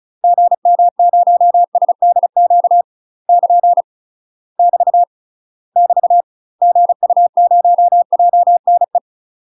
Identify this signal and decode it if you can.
Morse code, heard as a keyed tone, about 34 wpm, 700 Hz. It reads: GM0SDQ <KN> <BT> <BT> GU0JNE
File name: UK MASTER QSOs TRAINING / 35WPM / UK MASTER QSOs 0036